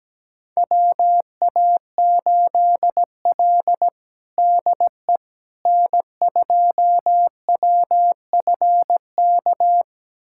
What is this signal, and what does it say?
Morse code, heard as a keyed tone, about 17 wpm, 700 Hz. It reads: WA8L DE N2WFK
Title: USA Callsign DE Callsign 17wpm